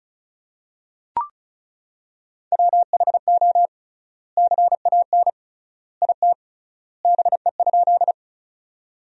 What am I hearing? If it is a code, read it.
Morse code, heard as a keyed tone, about 36 wpm, 700 Hz. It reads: WHO CAN IT BE?
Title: 3x Word-Spacing - Sentences from Top 100 Words - 35wpm